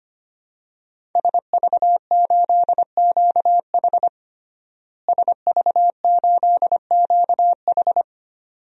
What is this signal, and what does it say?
Morse code, heard as a keyed tone, about 25 wpm, 700 Hz. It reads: SV8QH SV8QH
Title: DXCC-100 Call Signs - 5 Characters - Repeated 2x - 25wpm